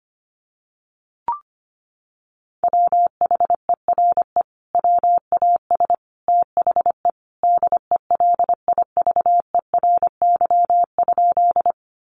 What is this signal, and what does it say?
Morse code, heard as a keyed tone, about 25 wpm, 700 Hz. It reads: WHERE WAS THE DELIVERY?
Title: Sentences from Top 1500 Words 25wpm